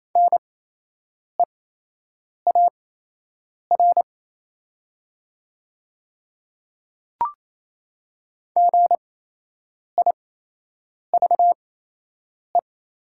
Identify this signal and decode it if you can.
Morse code, heard as a keyed tone, about 28 wpm, spaced as for 8 wpm, 700 Hz. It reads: NEAR GIVE
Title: 8x Character Spacing - Top 200 Words - 28wpm